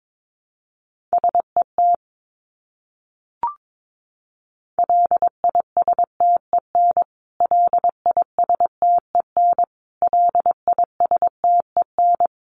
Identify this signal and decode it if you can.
Morse code, heard as a keyed tone, about 22 wpm, 700 Hz. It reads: SET LISTEN LISTEN LISTEN
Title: Top 500 Words Repeated 22wpm